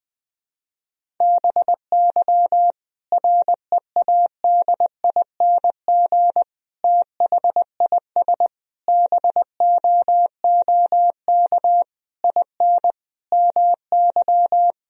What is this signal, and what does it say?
Morse code, heard as a keyed tone, about 20 wpm, 700 Hz. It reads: BY READING THIS BOOK IN MY
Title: Sentences from Top 1000 Words - Review Part 3 - 20wpm